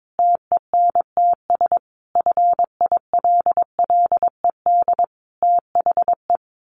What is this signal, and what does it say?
Morse code, heard as a keyed tone, about 22 wpm, 700 Hz. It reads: TENTS FILLED THE
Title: A Sound Of Thunder 22wpm